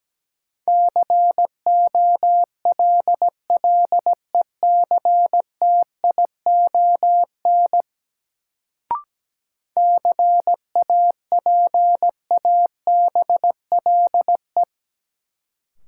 Morse code, heard as a keyed tone, about 17 wpm, 700 Hz. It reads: COLLECTION CAPABLE
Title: Sentences from Top 1200 Words - New Word Drill - 17wpm